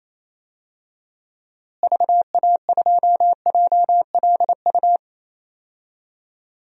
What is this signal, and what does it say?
Morse code, heard as a keyed tone, about 28 wpm, 700 Hz. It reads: VA2JLU
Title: DXCC-100 Call Signs - 6 Characters - Repeated 3x - 28wpm